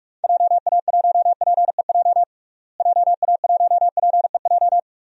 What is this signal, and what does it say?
Morse code, heard as a keyed tone, about 46 wpm, 700 Hz. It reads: JA1PEJ JA1PEJ
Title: DXCC-100 Call Signs - 6 Characters - Repeated 3x - 45wpm